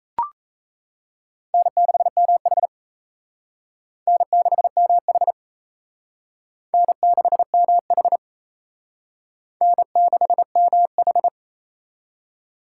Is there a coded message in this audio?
Yes: Morse code from 1.5 to 11.3 s, heard as a keyed tone, about 36 wpm, 700 Hz. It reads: N6MH N6MH N6MH N6MH